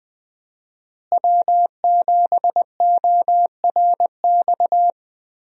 Morse code, heard as a keyed tone, about 20 wpm, 700 Hz. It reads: W7ORX
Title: USA Call Signs - 5 Characters - Repeated 2x - 20wpm